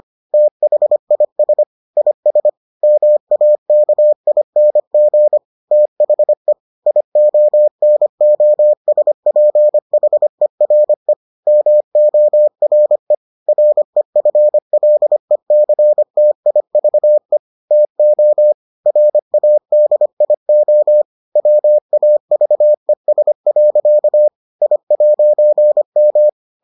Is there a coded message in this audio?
Yes: Morse code from 0.3 to 26.3 s, heard as a keyed tone, about 25 wpm, 600 Hz. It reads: THIS IS MAKING THE IONOSPHERE MORE REFLECTIVE TO RADIO WAVES. I'M